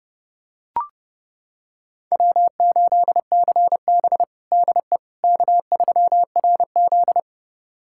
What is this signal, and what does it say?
Morse code, heard as a keyed tone, about 30 wpm, 700 Hz. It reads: W8CB DE K3RZ
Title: USA Callsign DE Callsign - 4 Characters - 30wpm